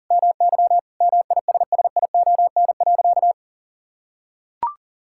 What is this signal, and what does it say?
Morse code, heard as a keyed tone, about 37 wpm, 700 Hz. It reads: MY MISSION.
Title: Sentences from Top 1600 Words 40wpm